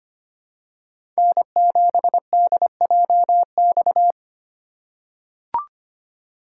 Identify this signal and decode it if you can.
Morse code, heard as a keyed tone, about 25 wpm, 700 Hz. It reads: N7DJX
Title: USA Call Signs - 5 Characters - 2x Character Spacing - 25wpm